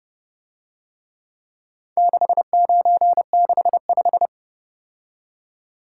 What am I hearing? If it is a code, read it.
Morse code, heard as a keyed tone, about 30 wpm, 700 Hz. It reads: 6965